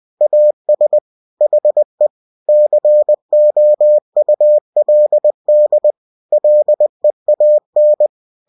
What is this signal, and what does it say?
Morse code, heard as a keyed tone, about 20 wpm, 600 Hz. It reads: AS HE COULD LEAN
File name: cw-books / winnie-the-pooh / winnie-20wpm-page-0092